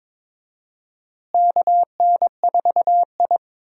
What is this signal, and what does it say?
Morse code, heard as a keyed tone, about 22 wpm, 700 Hz. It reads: KN4I
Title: USA Call Signs - 4 Characters - Repeated 3x - 22wpm